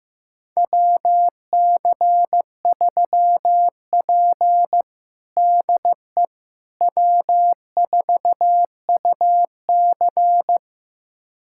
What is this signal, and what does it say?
Morse code, heard as a keyed tone, about 15 wpm, 700 Hz. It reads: WC3P DE W4UC
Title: USA Callsign DE Callsign - 4 Characters - 15wpm